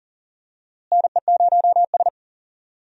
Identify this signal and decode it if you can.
Morse code, heard as a keyed tone, about 40 wpm, 700 Hz. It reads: NE0S
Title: USA Call Signs - 4 Characters - Repeated 3x - 40wpm